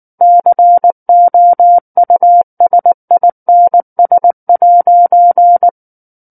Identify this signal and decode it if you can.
Morse code, heard as a keyed tone, about 19 wpm, 700 Hz. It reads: COUSINS'
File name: Book One / 10-Chapter 10 - 19 WPM